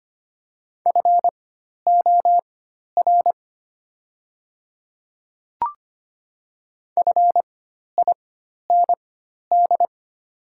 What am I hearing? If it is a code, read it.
Morse code, heard as a keyed tone, about 25 wpm, spaced as for 12 wpm, 700 Hz. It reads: FOR FIND
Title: 4x Character Spacing - Top 100 Words - 25wpm